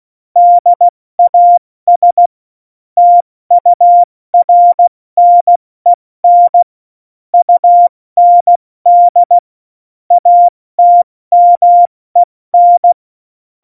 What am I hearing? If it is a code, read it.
Morse code, heard as a keyed tone, about 16 wpm, 700 Hz. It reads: DAS TURNEN UND ATMEN
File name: Morsekurs / Witze / Witz 19. Dezember 2017, 70 F80